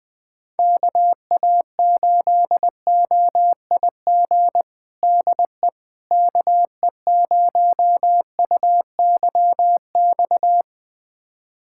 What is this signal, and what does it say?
Morse code, heard as a keyed tone, about 21 wpm, 700 Hz. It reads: KA8OIG DE KE0UYX
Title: USA Callsign DE Callsign - 6 Characters - 20wpm